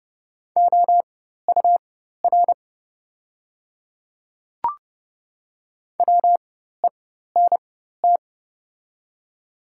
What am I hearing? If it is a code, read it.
Morse code, heard as a keyed tone, about 30 wpm, spaced as for 14 wpm, 690 Hz. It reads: OUR WENT
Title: 4x Character Spacing - Top 200 Words - 30wpm